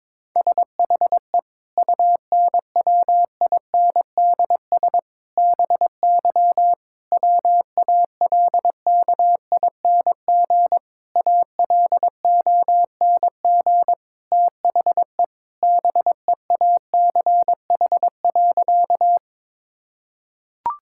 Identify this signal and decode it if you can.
Morse code, heard as a keyed tone, about 22 wpm, 700 Hz. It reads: SHE UNWINDS BY WALKING ALONG THE BEACH.
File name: Difficult Sentences with Hints - 22wpm